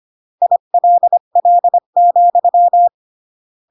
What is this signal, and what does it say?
Morse code, heard as a keyed tone, about 25 wpm, 700 Hz. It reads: ILL,